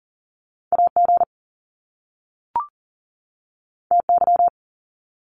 Morse code, heard as a keyed tone, about 39 wpm, 700 Hz. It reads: AG TY
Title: Top Two Letter Combinations in Words 40wpm